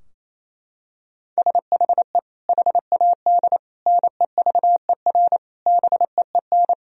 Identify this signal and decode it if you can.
Morse code, heard as a keyed tone, about 27 wpm, 700 Hz. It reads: SHE HAD NEVER BEEN
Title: Sentences from Top 1000 Words - Review Part 2 - 28wpm